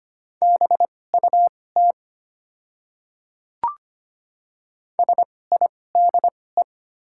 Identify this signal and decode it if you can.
Morse code, heard as a keyed tone, about 25 wpm, spaced as for 18 wpm, 700 Hz. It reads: BUT SIDE
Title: 2x Character Spacing - Top 100 Words - 25wpm